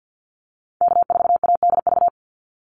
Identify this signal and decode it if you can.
Morse code, heard as a keyed tone, about 50 wpm, 700 Hz. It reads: K4ADV
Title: NA SOTA Activator Callsigns 50wpm